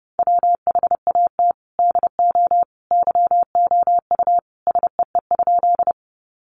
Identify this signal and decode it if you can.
Morse code, heard as a keyed tone, about 30 wpm, 700 Hz. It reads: WHAT DO YOU SEE?
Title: Sentences from Top 100 Words 30wpm